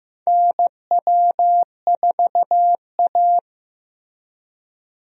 Morse code, heard as a keyed tone, about 15 wpm, 700 Hz. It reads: NW4A
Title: USA Call Signs - 4 Characters - Repeated 3x - 15wpm